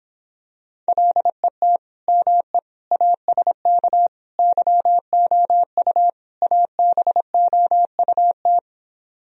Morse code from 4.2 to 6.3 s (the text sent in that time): YOU